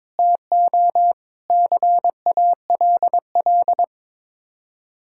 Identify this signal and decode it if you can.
Morse code, heard as a keyed tone, about 22 wpm, 700 Hz. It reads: TO CALL
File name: practice / 22wpm / Sets of 7 Words - 22wpm